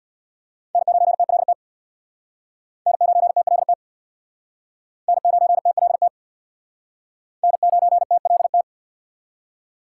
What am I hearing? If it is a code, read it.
Morse code, heard as a keyed tone, about 64 wpm, 700 Hz. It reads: N9TLT N9TLT N9TLT N9TLT